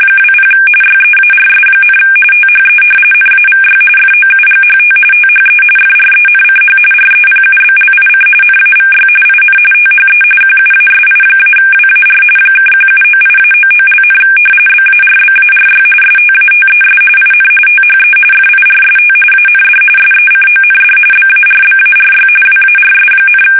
RTTY, about 45.45 baud. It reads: 20 YEARS, THE NAVY FOUGHT THE FRENCH REPUBLIC NAVY IN THE QUASI-WAR (1798-99), BARBARY STATES IN THE FIRST AND SECOND BARBARY WARS,
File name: fsk850 / usnavy1